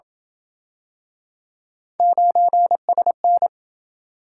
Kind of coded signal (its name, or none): Morse code